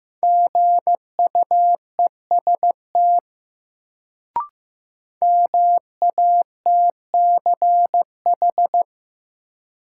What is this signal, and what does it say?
Morse code, heard as a keyed tone, about 15 wpm, 700 Hz. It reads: GUEST MATCH
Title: Sentences from Top 1600 Words - New Word Drill - 15wpm